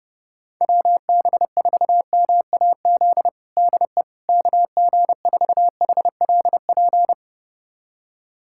Morse code, heard as a keyed tone, about 32 wpm, 700 Hz. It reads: WB4MAZ DE KG4HLP
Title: USA Callsign DE Callsign - 6 Characters - 30wpm